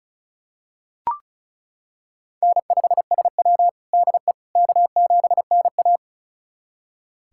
Morse code, heard as a keyed tone, about 34 wpm, 700 Hz. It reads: N5SW DE K7NA
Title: USA Callsign DE Callsign - 4 Characters - 35wpm